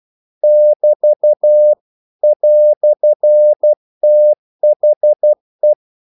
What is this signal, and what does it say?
Morse code, heard as a keyed tone, about 12 wpm, 600 Hz. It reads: <BT> "THE
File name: CW / Practice / EnglishQuotations0112